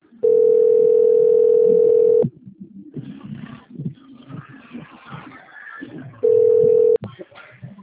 Ringback tone (North America), 440 + 480 Hz.